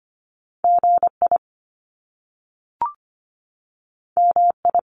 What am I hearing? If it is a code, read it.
Morse code, heard as a keyed tone, about 25 wpm, 700 Hz. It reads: GI MI
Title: Top Two Letter Combinations in Words 25wpm